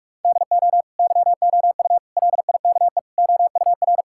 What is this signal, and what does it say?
Morse code, heard as a keyed tone, about 46 wpm, 700 Hz. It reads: DO YOU LIKE OUR
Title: Sentences from Top 1000 Words - Review Part 6 - 45wpm